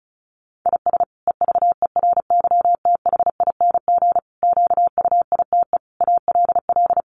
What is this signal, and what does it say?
Morse code, heard as a keyed tone, about 34 wpm, 700 Hz. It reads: IS EVERYTHING QUITE ALL
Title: Sentences from Top 700 Words 35wpm